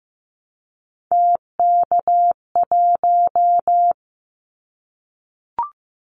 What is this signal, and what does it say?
Morse code, heard as a keyed tone, about 15 wpm, 700 Hz. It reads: TK1
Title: Callsign Prefixes for Top 100 DXCC - 15wpm